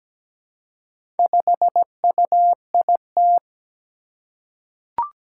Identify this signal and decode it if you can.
Morse code, heard as a keyed tone, about 17 wpm, 700 Hz. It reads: 5UIT
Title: Four Challenging Letter-Number Combinations 17wpm